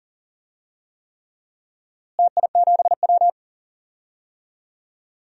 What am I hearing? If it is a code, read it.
Morse code, heard as a keyed tone, about 40 wpm, 700 Hz. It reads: TI7W
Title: DXCC-100 Call Signs - 4 Characters - Repeated 3x - 40wpm